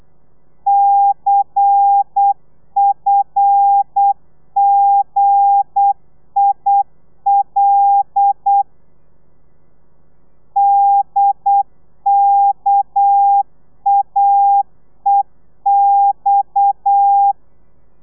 Morse code, heard as a keyed tone, about 8 wpm, 800 Hz. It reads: CFGIL DKAEX
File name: Lez11